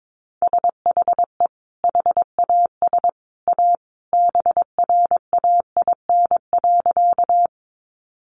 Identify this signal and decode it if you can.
Morse code, heard as a keyed tone, about 22 wpm, 700 Hz. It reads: SHE HAS A BRAIN.
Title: Sentences from Top 1100 Words 22wpm